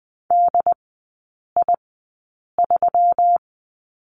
Morse code, heard as a keyed tone, about 20 wpm, spaced as for 8 wpm, 700 Hz. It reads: DI3
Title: 121 Lesson 20e - Number 3 - Three Characters 20wpm